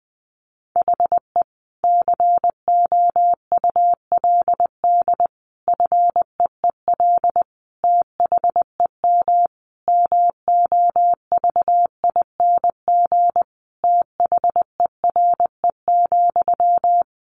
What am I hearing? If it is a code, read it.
Morse code, heard as a keyed tone, about 20 wpm, 700 Hz. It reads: HE COULD FEEL THEM MOVING THERE,
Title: A Sound Of Thunder 20wpm